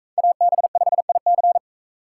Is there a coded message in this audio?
Yes: Morse code from 0.2 to 1.6 s, heard as a keyed tone, about 40 wpm, 700 Hz. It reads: AB5IC